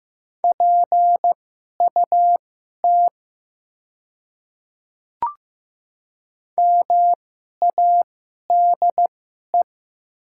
Morse code, heard as a keyed tone, about 15 wpm, spaced as for 11 wpm, 700 Hz. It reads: PUT MADE
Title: 2x Character Spacing - Top 200 Words - 15wpm